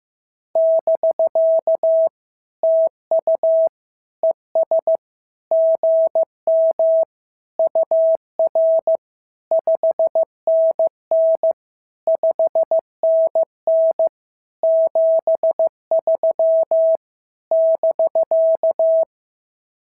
Morse code, heard as a keyed tone, about 15 wpm, 650 Hz. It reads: <BK> TU ES GM UR 5NN 5NN 73 <BK>
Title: SOTA QSO Practice 15wpm